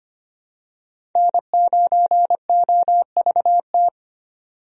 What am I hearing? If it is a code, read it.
Morse code, heard as a keyed tone, about 25 wpm, 700 Hz. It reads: N9OVT
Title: USA Call Signs - 5 Characters - 25wpm